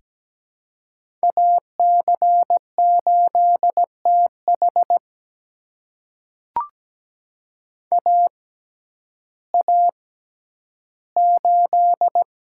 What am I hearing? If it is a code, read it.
Morse code, heard as a keyed tone, about 17 wpm, spaced as for 10 wpm, 700 Hz. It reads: AC8TH AA8
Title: USA Call Signs - 5 Characters - 6x Character Spacing - 17wpm